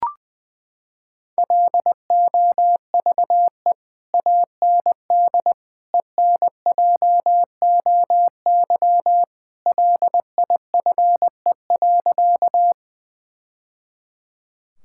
Morse code, heard as a keyed tone, about 20 wpm, 700 Hz. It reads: LOVE AND ENJOY LIFE.
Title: Sentences from Top 1000 Words - Review Part 5 - 20wpm